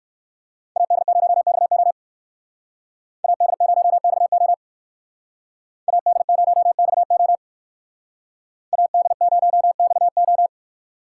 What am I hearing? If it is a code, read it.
Morse code, heard as a keyed tone, about 55 wpm, 700 Hz. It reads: AD0XO AD0XO AD0XO AD0XO
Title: Speed Racing - USA 2x2 Call Signs 45wpm